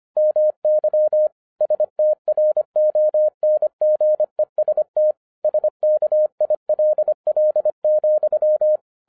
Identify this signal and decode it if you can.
Morse code, heard as a keyed tone, about 25 wpm, 610 Hz. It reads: MY STRONGEST SKILL,
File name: FIRST FISTS CONTACT-25wpm